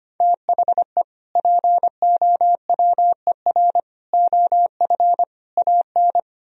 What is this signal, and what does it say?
Morse code, heard as a keyed tone, about 25 wpm, 700 Hz. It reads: THE POWER OF AN
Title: Sentences from Top 1000 Words - Review Part 4 - 25wpm